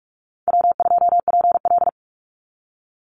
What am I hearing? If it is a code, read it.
Morse code, heard as a keyed tone, about 44 wpm, 700 Hz. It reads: W2PL
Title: Call Signs - 45wpm